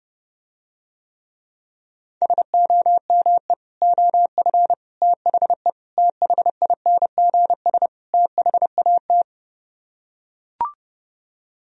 Morse code, heard as a keyed tone, about 30 wpm, 700 Hz. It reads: SOME OF THE THINGS THAT